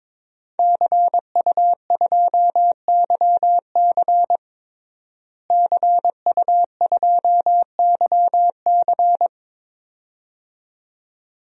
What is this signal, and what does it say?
Morse code, heard as a keyed tone, about 22 wpm, 700 Hz. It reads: CU2YC CU2YC
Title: DXCC-100 Call Signs - 5 Characters - Repeated 2x - 22wpm